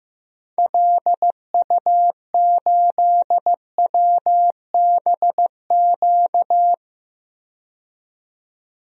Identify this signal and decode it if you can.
Morse code, heard as a keyed tone, about 15 wpm, 700 Hz. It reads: LU8WBQ
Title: DXCC-100 Call Signs - 6 Characters - 15wpm